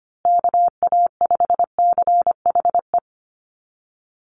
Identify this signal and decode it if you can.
Morse code, heard as a keyed tone, about 25 wpm, 700 Hz. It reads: KA5CHE
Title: USA Call Signs Repeated 3x 25wpm